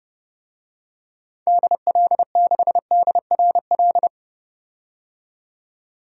Morse code, heard as a keyed tone, about 30 wpm, 700 Hz. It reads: DL6DRL
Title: DXCC-100 Call Signs - 6 Characters - 30wpm